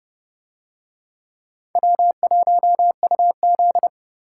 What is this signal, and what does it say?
Morse code, heard as a keyed tone, about 27 wpm, 700 Hz. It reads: W1UZ